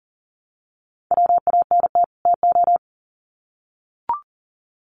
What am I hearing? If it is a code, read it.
Morse code, heard as a keyed tone, about 40 wpm, 700 Hz. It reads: WANT TO
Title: Sets of 2 Words 40wpm